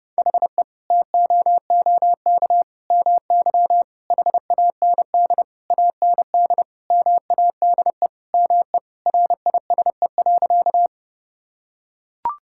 Morse code, heard as a keyed tone, about 30 wpm, 700 Hz. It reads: HE TOOK MY HAND AND MADE ME RISE.
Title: Sentences from Top 1000 Words - Review Part 6 - 30wpm